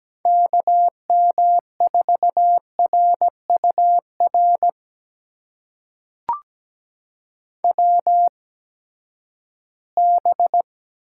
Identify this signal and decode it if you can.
Morse code, heard as a keyed tone, about 17 wpm, spaced as for 12 wpm, 700 Hz. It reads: KM4RUR WB
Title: USA Call Signs - 6 Characters - 8x Character Spacing - 17wpm